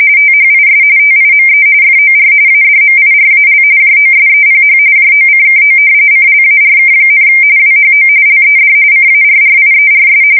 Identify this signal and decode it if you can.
RTTY, about 45.45 baud. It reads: THIS TIME FOR ADMIRAL JOSEPH JOCKO CLARK, WHO LED CARRIERS
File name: fsk170 / battleshipnj